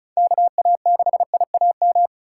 Morse code, heard as a keyed tone, about 36 wpm, 700 Hz. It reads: KA6IAM
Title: USA Call Signs - 6 Characters - 3x Character Spacing - 35wpm